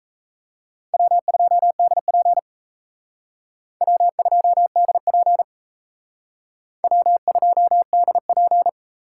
Morse code, heard as a keyed tone, about 36 wpm, 700 Hz. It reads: W2DP W2DP W2DP